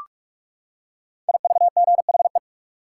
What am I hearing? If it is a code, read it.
Morse code, heard as a keyed tone, about 44 wpm, 700 Hz. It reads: IVGHE
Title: Code Groups - 5 Letters - 45wpm